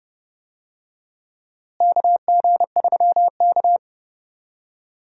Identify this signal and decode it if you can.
Morse code, heard as a keyed tone, about 32 wpm, 700 Hz. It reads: KG3K
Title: USA Call Signs - 4 Characters - Repeated 2x - 30wpm